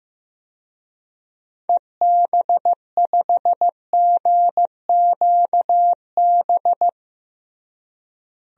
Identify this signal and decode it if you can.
Morse code, heard as a keyed tone, about 15 wpm, 700 Hz. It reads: EB5GQB
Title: DXCC-100 Call Signs - 6 Characters - Repeated 3x - 15wpm